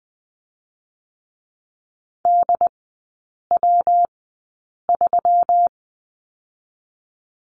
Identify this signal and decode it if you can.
Morse code, heard as a keyed tone, about 20 wpm, spaced as for 8 wpm, 700 Hz. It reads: DW3
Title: 121 Lesson 20e - Number 3 - Three Characters 20wpm